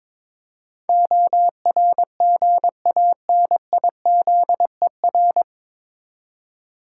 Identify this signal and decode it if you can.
Morse code, heard as a keyed tone, about 22 wpm, 700 Hz. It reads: ORGANIZER
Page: Morse Code Ninja practice 22wpm